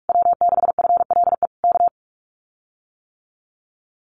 Morse code, heard as a keyed tone, about 44 wpm, 700 Hz. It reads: W6FLE K
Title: Common QSO Phrases Encore 45wpm